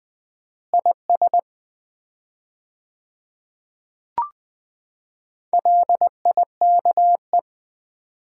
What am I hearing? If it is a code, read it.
Morse code, heard as a keyed tone, about 21 wpm, 700 Hz. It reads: IS LIKE
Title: 4x Word-Spacing - Sets of 2 Words - 20wpm